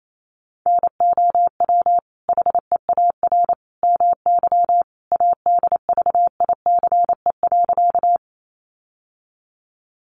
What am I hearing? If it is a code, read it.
Morse code, heard as a keyed tone, about 28 wpm, 700 Hz. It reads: NOW HEAR MY ADVICE.
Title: Sentences from Top 1100 Words 28wpm